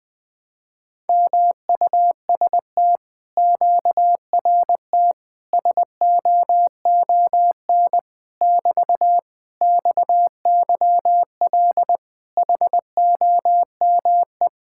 Morse code, heard as a keyed tone, about 20 wpm, 700 Hz. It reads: MUST QRT SOON <BT> XYL HOME
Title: Speed-Racing Common QSO Phrases - 17wpm